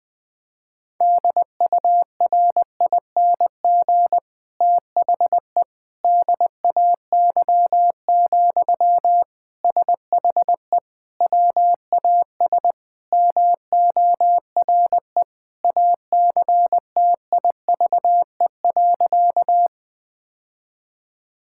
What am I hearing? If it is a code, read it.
Morse code, heard as a keyed tone, about 20 wpm, 700 Hz. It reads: DURING THE DAY, SHE WAS MORE ACTIVE.
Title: Sentences from Top 1000 Words - Review Part 6 - 20wpm